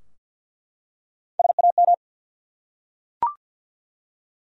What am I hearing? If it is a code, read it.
Morse code, heard as a keyed tone, about 48 wpm, 700 Hz. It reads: SAM
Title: USA Common Names - Max 3 Letters - 50wpm